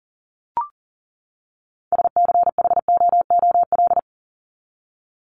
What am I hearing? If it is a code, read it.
Morse code, heard as a keyed tone, about 40 wpm, 700 Hz. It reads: SCHOOL
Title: Top 300 Words 40wpm